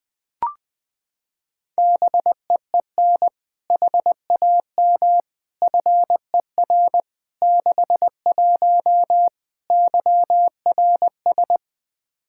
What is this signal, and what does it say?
Morse code, heard as a keyed tone, about 21 wpm, 700 Hz. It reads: BEEN HAM FER 61 YRS
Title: Practice QSO - Exchange 3 - More Info - 20wpm